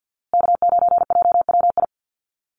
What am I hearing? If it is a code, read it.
Morse code, heard as a keyed tone, about 50 wpm, 700 Hz. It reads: K9JWI
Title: Call Signs Repeated 2x - 50wpm